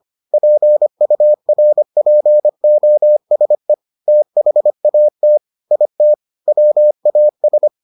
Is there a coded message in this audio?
Yes: Morse code from 0.3 to 7.7 s, heard as a keyed tone, about 25 wpm, 600 Hz. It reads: PURPOSE THAT IT WAS